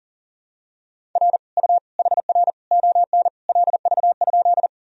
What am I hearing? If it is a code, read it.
Morse code, heard as a keyed tone, about 37 wpm, 700 Hz. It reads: R U HR ON LV?
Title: Sentences from Top 400 Words and Abbreviations 40wpm